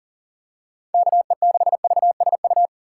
Morse code, heard as a keyed tone, about 38 wpm, 700 Hz. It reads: KE6VSU
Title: USA Call Signs - 6 Characters - 4x Character Spacing - 40wpm